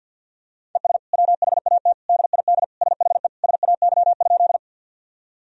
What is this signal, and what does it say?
Morse code, heard as a keyed tone, about 52 wpm, 700 Hz. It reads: ES WHAT DID SHE SAY?
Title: Sentences from Top 200 Words and Abbreviations 50wpm